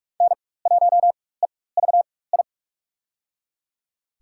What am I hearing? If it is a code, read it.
Morse code, heard as a keyed tone, about 45 wpm, spaced as for 21 wpm, 700 Hz. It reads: N1EVI
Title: USA Call Signs - 5 Characters - 4x Character Spacing - 45wpm